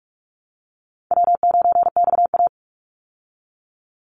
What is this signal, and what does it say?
Morse code, heard as a keyed tone, about 44 wpm, 700 Hz. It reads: W9XA
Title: Call Signs Repeated 2x - 45wpm